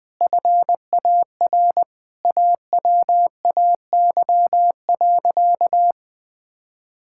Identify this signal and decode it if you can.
Morse code, heard as a keyed tone, about 21 wpm, 700 Hz. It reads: FAR AWAY.
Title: Sentences with 5-Word Phrases - 20wpm